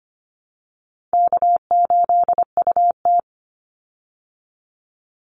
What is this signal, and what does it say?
Morse code, heard as a keyed tone, about 25 wpm, 700 Hz. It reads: K8UT
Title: Easy Call Signs Repeated 2x - 25wpm